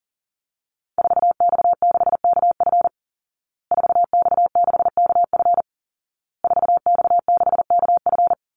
Morse code, heard as a keyed tone, about 40 wpm, 700 Hz. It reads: 4X6KF 4X6KF 4X6KF
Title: Call Signs Repeated 2x - 40wpm